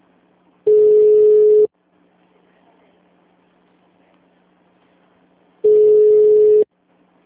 Ringback tone (Europe, 425 Hz).